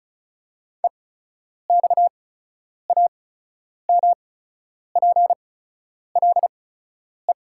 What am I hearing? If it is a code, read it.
Morse code, heard as a keyed tone, about 34 wpm, spaced as for 10 wpm, 700 Hz. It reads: EXAMPLE
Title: 8x Character Spacing - Top 300 Words - 35wpm